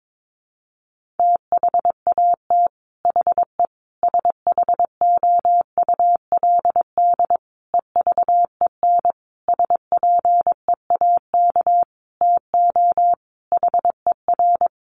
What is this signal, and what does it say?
Morse code, heard as a keyed tone, about 22 wpm, 700 Hz. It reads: THAT HE SHOULD EVEN SPEAK TO HER
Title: Sentences from Top 1500 Words 22wpm